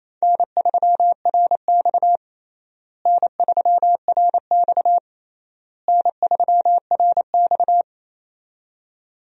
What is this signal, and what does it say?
Morse code, heard as a keyed tone, about 28 wpm, 700 Hz. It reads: N3RX N3RX N3RX
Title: USA Call Signs - 4 Characters - Repeated 3x - 28wpm